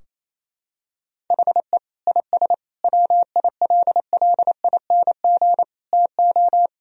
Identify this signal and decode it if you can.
Morse code, heard as a keyed tone, about 28 wpm, 700 Hz. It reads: HE IS WILLING TO
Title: Sentences from Top 1000 Words - Review Part 4 - 28wpm